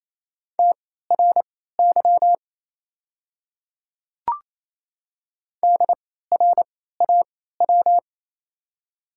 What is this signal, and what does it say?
Morse code, heard as a keyed tone, about 28 wpm, spaced as for 16 wpm, 700 Hz. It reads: TRY DRAW
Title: 3x Character Spacing - Top 500 Words - 28wpm